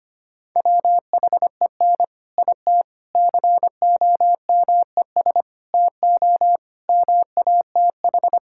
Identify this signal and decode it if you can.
Morse code, heard as a keyed tone, about 25 wpm, 700 Hz. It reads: WHEN IT COMES TO MATH
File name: practice / 25wpm / Sentences with 4-Word Phrases - 25wpm